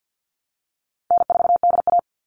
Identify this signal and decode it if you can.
Morse code, heard as a keyed tone, about 48 wpm, 700 Hz. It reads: N4DA
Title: NA SOTA Chaser Callsigns 50wpm